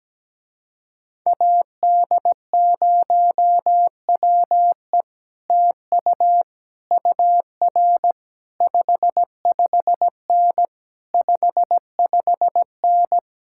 Morse code, heard as a keyed tone, about 17 wpm, 700 Hz. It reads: AD0WE TU UR 55N 55N
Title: Speed-Racing Common QSO Phrases - 17wpm